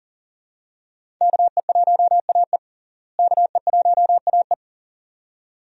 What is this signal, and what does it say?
Morse code, heard as a keyed tone, about 40 wpm, 700 Hz. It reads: KE1AE KE1AE
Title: USA Call Signs - 5 Characters - Repeated 2x - 40wpm